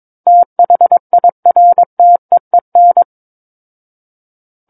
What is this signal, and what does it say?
Morse code, heard as a keyed tone, about 22 wpm, 700 Hz. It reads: THIRTEEN
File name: Book One / 13-Chapter 13 - 22 WPM